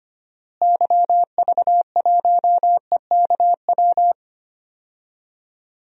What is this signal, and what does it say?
Morse code, heard as a keyed tone, about 25 wpm, 700 Hz. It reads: YV1EKW
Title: DXCC-100 Call Signs - 6 Characters - Repeated 3x - 25wpm